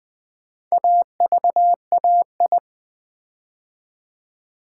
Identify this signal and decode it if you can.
Morse code, heard as a keyed tone, about 20 wpm, 700 Hz. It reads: AVAI
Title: Build A Word - 4 Letters At A Time - 12-Letter Words - 20wpm